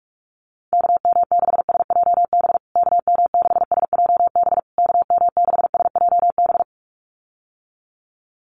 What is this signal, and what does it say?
Morse code, heard as a keyed tone, about 44 wpm, 700 Hz. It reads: KM6SJB KM6SJB KM6SJB
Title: USA Call Signs Repeated 3x 45wpm